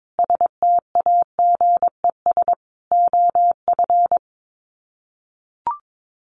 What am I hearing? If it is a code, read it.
Morse code, heard as a keyed tone, about 22 wpm, 700 Hz. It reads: STAGES OF
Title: Sets of 5 Words Encore 22wpm